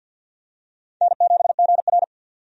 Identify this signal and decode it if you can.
Morse code, heard as a keyed tone, about 47 wpm, 700 Hz. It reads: N7GR
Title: USA Call Signs - 4 Characters - 50wpm